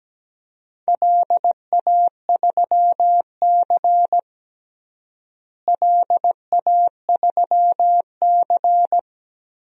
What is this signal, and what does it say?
Morse code, heard as a keyed tone, about 17 wpm, 700 Hz. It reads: LA3C LA3C
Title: DXCC-100 Call Signs - 4 Characters - Repeated 3x - 17wpm